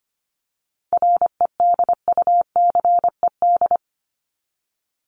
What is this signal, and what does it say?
Morse code, heard as a keyed tone, about 25 wpm, 700 Hz. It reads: REDUCED
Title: ED Suffix - Top 100 Words with Base Words 25wpm